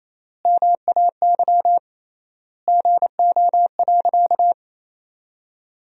Morse code, heard as a keyed tone, about 27 wpm, 700 Hz. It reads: MAY GO.
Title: 3x Word-Spacing - Sentences from Top 100 Words - 28wpm